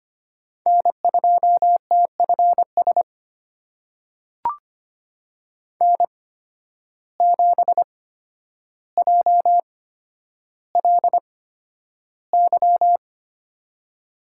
Morse code, heard as a keyed tone, about 25 wpm, spaced as for 11 wpm, 700 Hz. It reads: N2TFS N7JLY
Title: USA Call Signs - 5 Characters - 8x Character Spacing - 25wpm